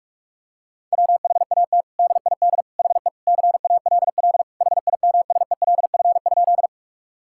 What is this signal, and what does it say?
Morse code, heard as a keyed tone, about 46 wpm, 700 Hz. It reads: WHAT DID HE CALL HIMSELF?
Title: Sentences from Top 1600 Words 45wpm